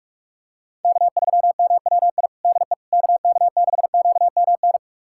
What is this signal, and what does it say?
Morse code, heard as a keyed tone, about 44 wpm, 700 Hz. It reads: K3MWI DE KK6QMN
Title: USA Callsign DE Callsign 45wpm